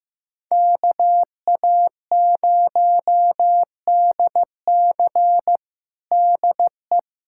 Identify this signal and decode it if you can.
Morse code, heard as a keyed tone, about 15 wpm, 700 Hz. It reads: KA0DC DE
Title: Practice QSO - Exchange 4 - Ending - 15wpm